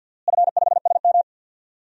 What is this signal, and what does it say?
Morse code, heard as a keyed tone, about 52 wpm, 700 Hz. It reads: F5SM
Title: DXCC-100 Call Signs - 4 Characters - Repeated 2x - 50wpm